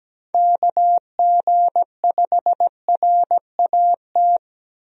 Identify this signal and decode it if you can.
Morse code, heard as a keyed tone, about 17 wpm, 700 Hz. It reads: KG5RAT
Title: USA Call Signs - 6 Characters - 17wpm